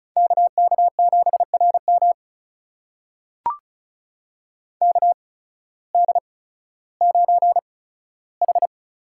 Morse code, heard as a keyed tone, about 36 wpm, spaced as for 16 wpm, 700 Hz. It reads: KK7RM KD9H
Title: USA Call Signs - 8x Character Spacing - 35wpm